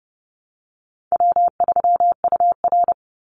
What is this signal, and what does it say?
Morse code, heard as a keyed tone, about 30 wpm, 700 Hz. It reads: W3UR
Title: Call Signs Repeated 3x - 30wpm